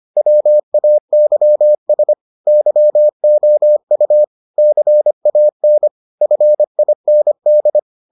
Morse code, heard as a keyed tone, about 25 wpm, 600 Hz. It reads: WAYS YOU CAN FIND